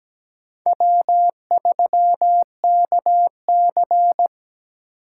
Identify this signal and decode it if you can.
Morse code, heard as a keyed tone, about 20 wpm, 700 Hz. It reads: W3KC
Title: Speed Racing - USA Call Signs 17wpm